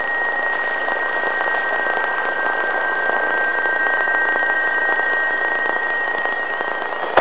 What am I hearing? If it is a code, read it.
PSK31, about 31.25 baud, with radio noise.